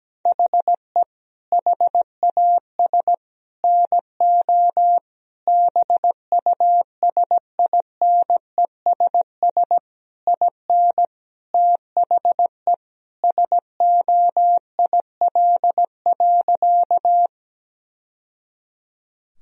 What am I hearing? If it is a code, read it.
Morse code, heard as a keyed tone, about 17 wpm, 700 Hz. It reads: HE HAS NO BUSINESS IN THE SOIL.
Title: Sentences from Top 1000 Words - Review Part 4 - 17wpm